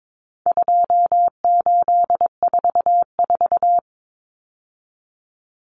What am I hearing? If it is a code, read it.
Morse code, heard as a keyed tone, about 22 wpm, 700 Hz. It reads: 2844
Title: Numbers - 4-Digit 22wpm